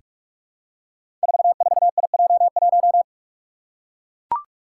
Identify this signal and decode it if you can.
Morse code, heard as a keyed tone, about 42 wpm, 700 Hz. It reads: 44IJ1
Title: Five Challenging Letter-Number Combinations 45wpm